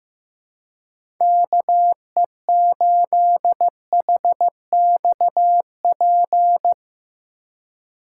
Morse code, heard as a keyed tone, about 15 wpm, 700 Hz. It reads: KE8HXP
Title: USA Call Signs - 2x Character Spacing - 15wpm